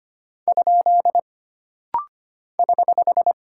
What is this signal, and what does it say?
Morse code, heard as a keyed tone, about 25 wpm, 700 Hz. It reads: ? <HH>
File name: Single Punctuation-Prosign - Rapid-Fire - 25wpm